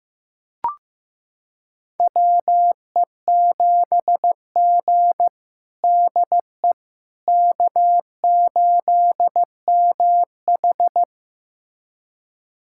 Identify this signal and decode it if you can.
Morse code, heard as a keyed tone, about 16 wpm, 700 Hz. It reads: WE7G DE K8MH
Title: USA Callsign DE Callsign - 4 Characters - 15wpm